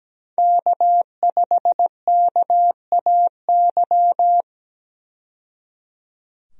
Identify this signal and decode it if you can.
Morse code, heard as a keyed tone, about 17 wpm, 700 Hz. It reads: K5KAY
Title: USA Call Signs - 5 Characters - 17wpm